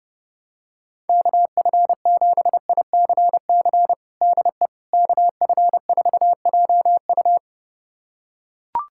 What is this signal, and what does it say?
Morse code, heard as a keyed tone, about 30 wpm, 700 Hz. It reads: KF7ICC DE KF4JU
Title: USA Callsign DE Callsign 30wpm